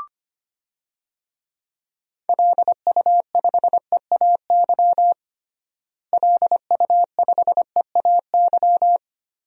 Morse code, heard as a keyed tone, about 25 wpm, 700 Hz. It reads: LU5EAY LU5EAY
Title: DXCC-100 Call Signs - 6 Characters - Repeated 3x - 25wpm